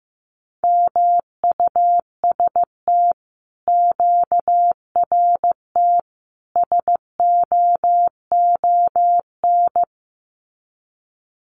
Morse code, heard as a keyed tone, about 15 wpm, 700 Hz. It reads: MUST QRT SOON
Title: Common QSO Phrases Encore 15wpm